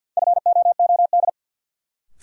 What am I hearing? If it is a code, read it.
Morse code, heard as a keyed tone, about 51 wpm, 700 Hz. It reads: FOOD
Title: Sets of 5 Words - New Word Drill - 50wpm